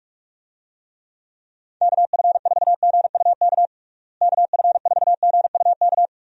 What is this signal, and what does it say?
Morse code, heard as a keyed tone, about 41 wpm, 700 Hz. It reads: KF4GUK KF4GUK
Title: USA Call Signs - 6 Characters - Repeated 2x - 45wpm